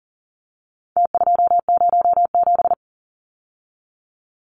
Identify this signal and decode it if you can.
Morse code, heard as a keyed tone, about 39 wpm, 700 Hz. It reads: T207
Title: Numbers - Contest Style - 4-Digit 40wpm